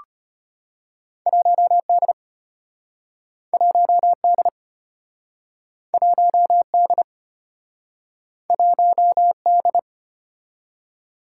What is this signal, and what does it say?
Morse code, heard as a keyed tone, about 34 wpm, 700 Hz. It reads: 1D 1D 1D 1D